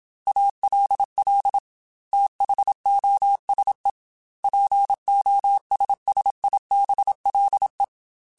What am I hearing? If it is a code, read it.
Morse code, heard as a keyed tone, about 27 wpm, 800 Hz. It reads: ALL THOSE POSSIBLE
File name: Technical Subjects / Technical Subjects-mp3s / AND Gate @25